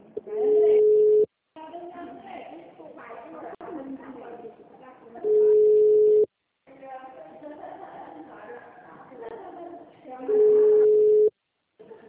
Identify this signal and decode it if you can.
Ringback tone (Europe, 425 Hz), with speech.
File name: out- -2016- - - .73560